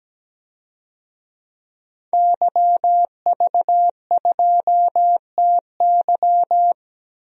Morse code, heard as a keyed tone, about 17 wpm, 700 Hz. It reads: YV2TY
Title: DXCC-100 Call Signs - 5 Characters - Repeated 3x - 17wpm